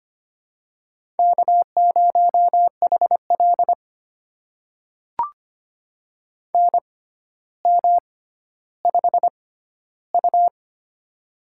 Morse code, heard as a keyed tone, about 25 wpm, spaced as for 13 wpm, 700 Hz. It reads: K0HL NM5U
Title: USA Call Signs - 4 Characters - 6x Character Spacing - 25wpm